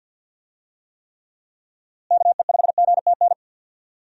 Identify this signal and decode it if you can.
Morse code, heard as a keyed tone, about 50 wpm, 700 Hz. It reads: KE5GTN
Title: USA Call Signs - 6 Characters - Repeated 2x - 50wpm